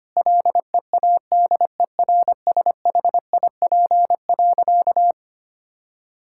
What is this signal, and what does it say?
Morse code, heard as a keyed tone, about 25 wpm, 700 Hz. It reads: LEADERSHIP.
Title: Sentences from Top 1600 Words 25wpm